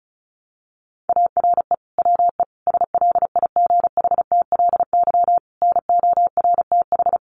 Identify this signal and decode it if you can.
Morse code, heard as a keyed tone, about 34 wpm, 700 Hz. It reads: ARE WE SLIGHTLY NORTH
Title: Sentences from Top 1300 Words 35wpm